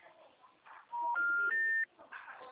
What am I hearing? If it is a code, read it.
Special information tone.
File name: out- -2003- - - .10310